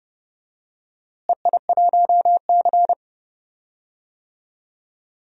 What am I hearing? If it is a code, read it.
Morse code, heard as a keyed tone, about 30 wpm, 700 Hz. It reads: EI1C
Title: DXCC-100 Call Signs - 4 Characters - Repeated 2x - 30wpm